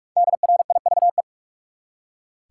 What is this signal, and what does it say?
Morse code, heard as a keyed tone, about 46 wpm, 700 Hz. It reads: DRIVE
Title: Five Letter Words - 45wpm